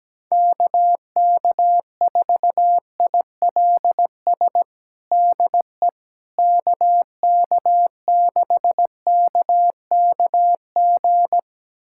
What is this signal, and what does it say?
Morse code, heard as a keyed tone, about 17 wpm, 700 Hz. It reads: KK4ILS DE KK6KKG
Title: USA Callsign DE Callsign - 6 Characters - 17wpm